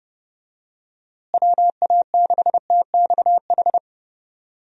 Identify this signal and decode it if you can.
Morse code, heard as a keyed tone, about 32 wpm, 700 Hz. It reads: WA6TXH
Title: USA Call Signs - 6 Characters - Repeated 2x - 30wpm